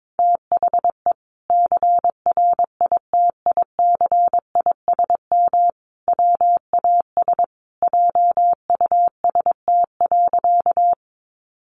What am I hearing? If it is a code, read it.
Morse code, heard as a keyed tone, about 22 wpm, 700 Hz. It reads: THE CRITICISM WAS JUST.
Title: Sentences from Top 1300 Words 22wpm